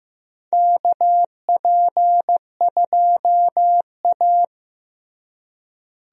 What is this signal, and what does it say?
Morse code, heard as a keyed tone, about 15 wpm, 700 Hz. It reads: KP2A
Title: DXCC-100 Call Signs - 4 Characters - Repeated 3x - 15wpm